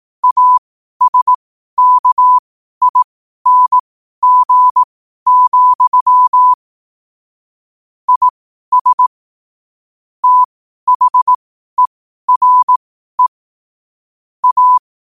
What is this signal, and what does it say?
Morse code, heard as a keyed tone, about 18 wpm, spaced as for 12 wpm, 1000 Hz. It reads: ASKING, IS THERE A
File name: qotdfiles / 12wpm